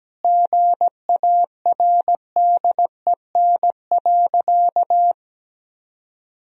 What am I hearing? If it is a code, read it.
Morse code, heard as a keyed tone, about 17 wpm, 700 Hz. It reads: GARDEN.
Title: Sentences from Top 1000 Words - Review Part 4 - 17wpm